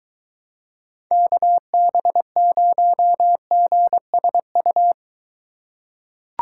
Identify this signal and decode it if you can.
Morse code, heard as a keyed tone, about 21 wpm, 700 Hz. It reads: KB0GSU